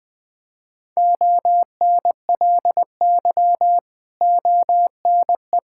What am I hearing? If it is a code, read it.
Morse code, heard as a keyed tone, about 20 wpm, 700 Hz. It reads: ONLY ONE
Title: Sentences from Top 1000 Words - Review Part 6 - 20wpm